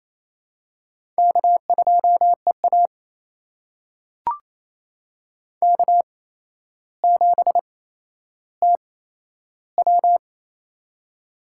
Morse code, heard as a keyed tone, about 27 wpm, spaced as for 12 wpm, 700 Hz. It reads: K2EA K7TW
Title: USA Call Signs - 4 Characters - 8x Character Spacing - 28wpm